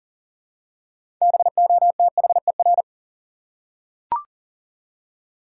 Morse code, heard as a keyed tone, about 40 wpm, 700 Hz. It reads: BOTHER
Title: Six Letter Words - 40wpm